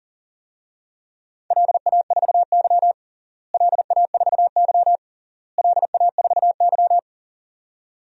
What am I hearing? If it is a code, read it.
Morse code, heard as a keyed tone, about 40 wpm, 700 Hz. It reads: LA4Y LA4Y LA4Y
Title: DXCC-100 Call Signs - 4 Characters - Repeated 3x - 40wpm